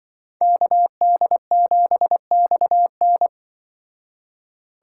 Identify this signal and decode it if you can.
Morse code, heard as a keyed tone, about 23 wpm, 700 Hz. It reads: KD7XN
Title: Speed Racing - USA 2x2 Call Signs 20wpm